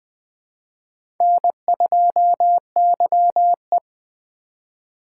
Morse code, heard as a keyed tone, about 20 wpm, 700 Hz. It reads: N2YE
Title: Speed Racing - USA Call Signs 17wpm